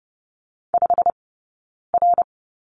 Morse code, heard as a keyed tone, about 30 wpm, spaced as for 9 wpm, 690 Hz. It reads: 5R
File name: Lesson 19d - Letter w - Two Characters 30wpm